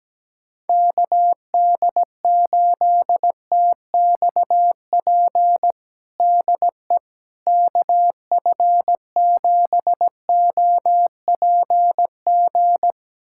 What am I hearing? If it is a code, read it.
Morse code, heard as a keyed tone, about 17 wpm, 700 Hz. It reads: KD8TXP DE KF7OPG
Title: USA Callsign DE Callsign - 6 Characters - 17wpm